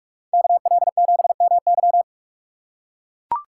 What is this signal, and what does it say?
Morse code, heard as a keyed tone, about 42 wpm, 700 Hz. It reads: KL7MY
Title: USA Call Signs - 5 Characters - 4x Character Spacing - 45wpm